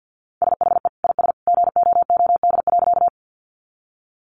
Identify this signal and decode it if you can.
Morse code, heard as a keyed tone, about 50 wpm, 700 Hz. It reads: SHE IS GOOD.
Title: Easy Phrases 50wpm